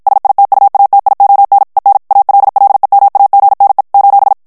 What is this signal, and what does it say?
Morse code, heard as a keyed tone, about 56 wpm, 800 Hz. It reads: SITUATION EN ALLEMAGNE :
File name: morse mp3 / FRANCAIS TEXTE CLAIR 1 VITESSE 50 MOTS MINUTES